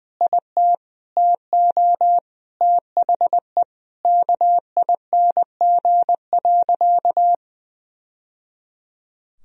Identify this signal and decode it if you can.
Morse code, heard as a keyed tone, about 19 wpm, 700 Hz. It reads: IT TO THE KING.
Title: Sentences from Top 1000 Words - Review Part 3 - 20wpm